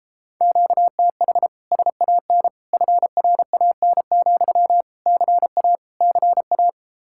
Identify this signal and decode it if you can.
Morse code, heard as a keyed tone, about 34 wpm, 700 Hz. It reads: QTH SAN FRAN, CA CA